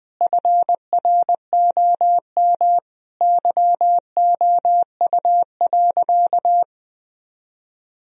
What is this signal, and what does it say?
Morse code, heard as a keyed tone, about 20 wpm, 700 Hz. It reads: FROM YOU.
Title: Sentences from Top 1000 Words - Review Part 2 - 20wpm